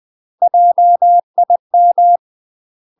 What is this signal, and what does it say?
Morse code, heard as a keyed tone, about 20 wpm, 700 Hz. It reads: JIM
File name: UK MASTER QSOs TRAINING / 20WPM / UK MASTER QSOs 0067